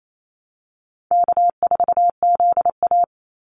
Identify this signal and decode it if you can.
Morse code, heard as a keyed tone, about 28 wpm, 700 Hz. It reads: K4ZA
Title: Easy Call Signs - 28wpm